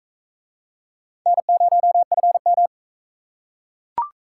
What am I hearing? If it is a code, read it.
Morse code, heard as a keyed tone, about 42 wpm, 700 Hz. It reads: N0FM